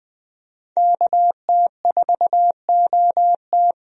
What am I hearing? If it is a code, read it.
Morse code, heard as a keyed tone, about 20 wpm, 700 Hz. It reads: KT4OT
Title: USA Call Signs - 5 Characters - 20wpm